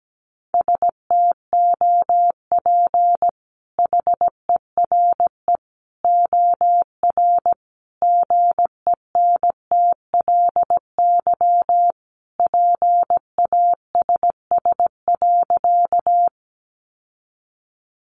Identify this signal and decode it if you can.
Morse code, heard as a keyed tone, about 17 wpm, 700 Hz. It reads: STOP HERE OR GENTLY PASS.
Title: Sentences from Top 1500 Words 17wpm